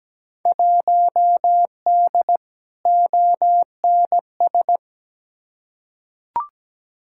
Morse code, heard as a keyed tone, about 17 wpm, 700 Hz. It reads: 1D ONS
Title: Field Day Exchanges - 17wpm